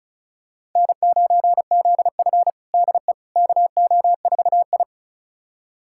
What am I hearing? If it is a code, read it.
Morse code, heard as a keyed tone, about 36 wpm, 700 Hz. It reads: N9ZF DE KO4I
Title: USA Callsign DE Callsign - 4 Characters - 35wpm